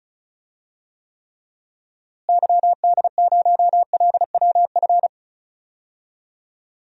Morse code, heard as a keyed tone, about 34 wpm, 700 Hz. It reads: YD0LWF
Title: DXCC-100 Call Signs - 6 Characters - Repeated 3x - 35wpm